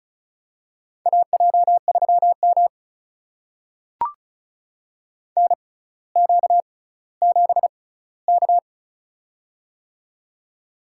Morse code, heard as a keyed tone, about 36 wpm, spaced as for 18 wpm, 700 Hz. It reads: AJ3M NQ7K
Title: USA Call Signs - 4 Characters - 6x Character Spacing - 35wpm